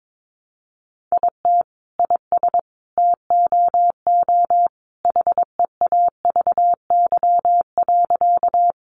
Morse code, heard as a keyed tone, about 22 wpm, 700 Hz. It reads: IT IS TOO HEAVY.
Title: Sentences from Top 1100 Words 22wpm